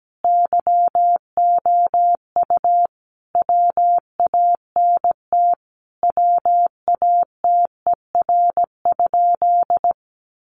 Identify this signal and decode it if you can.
Morse code, heard as a keyed tone, about 17 wpm, 700 Hz. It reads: YOU WANT WATER?
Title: Sentences from Top 200 Words 17wpm